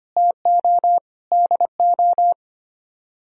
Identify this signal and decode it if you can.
Morse code, heard as a keyed tone, about 32 wpm, 700 Hz. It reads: TO DO